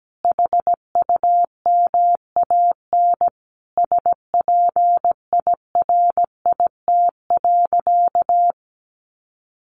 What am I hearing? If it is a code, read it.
Morse code, heard as a keyed tone, about 17 wpm, 700 Hz. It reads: HUMAN SPIRIT.
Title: Sentences from Top 1400 Words 17wpm